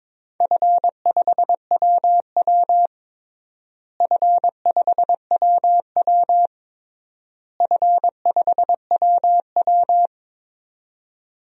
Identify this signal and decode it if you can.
Morse code, heard as a keyed tone, about 21 wpm, 700 Hz. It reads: F5WW F5WW F5WW
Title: DXCC-100 Call Signs - 4 Characters - Repeated 3x - 22wpm